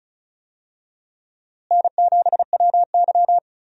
Morse code, heard as a keyed tone, about 36 wpm, 700 Hz. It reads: N7WY
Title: USA Call Signs - 4 Characters - Repeated 3x - 35wpm